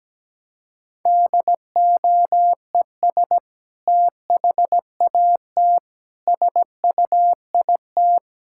Morse code, heard as a keyed tone, about 17 wpm, 700 Hz. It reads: DOES THAT SUIT
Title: Sentences from Top 1600 Words 17wpm